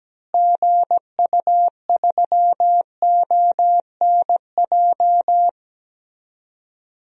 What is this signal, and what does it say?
Morse code, heard as a keyed tone, about 17 wpm, 700 Hz. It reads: GU3ONJ
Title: DXCC-100 Call Signs - 6 Characters - Repeated 2x - 17wpm